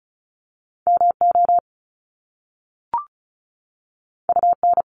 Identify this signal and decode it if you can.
Morse code, heard as a keyed tone, about 36 wpm, 700 Hz. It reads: MO UN
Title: Top Two Letter Combinations in Words 35wpm